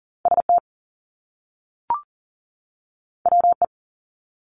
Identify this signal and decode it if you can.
Morse code, heard as a keyed tone, about 38 wpm, 700 Hz. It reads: ST WE
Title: Top Two Letter Combinations in Words 40wpm